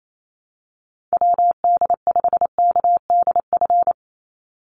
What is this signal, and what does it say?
Morse code, heard as a keyed tone, about 28 wpm, 700 Hz. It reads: WD5KDF
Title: USA Call Signs 28wpm